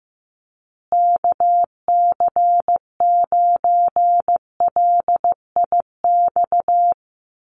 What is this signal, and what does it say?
Morse code, heard as a keyed tone, about 15 wpm, 700 Hz. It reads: KC9LIX
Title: USA Call Signs Repeated 3x 15wpm